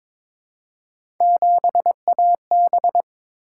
Morse code, heard as a keyed tone, about 22 wpm, 700 Hz. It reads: 7AB
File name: practice / 22wpm / Three Letter-Number Combinations - 22wpm